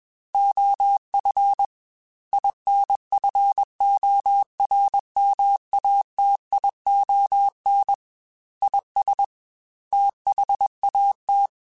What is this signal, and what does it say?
Morse code, heard as a keyed tone, about 21 wpm, 800 Hz. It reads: OF INFORMATION IS THAT
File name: Technical Subjects / Technical Subjects-mp3s / Sw Pwr Sup @20